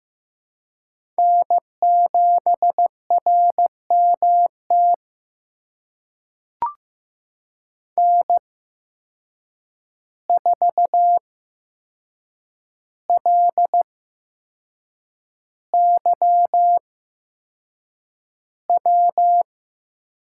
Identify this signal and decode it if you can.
Morse code, heard as a keyed tone, about 15 wpm, spaced as for 6 wpm, 700 Hz. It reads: N7RMT N4LYW
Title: USA Call Signs - 5 Characters - 8x Character Spacing - 15wpm